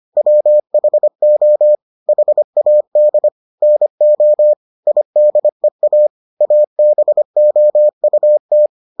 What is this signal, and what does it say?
Morse code, heard as a keyed tone, about 25 wpm, 600 Hz. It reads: WHO HAD NO IDEA ABOUT